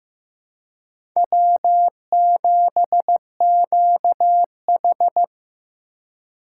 Morse code, heard as a keyed tone, about 15 wpm, 700 Hz. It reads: W7QH
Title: USA Call Signs - 4 Characters - 3x Character Spacing - 15wpm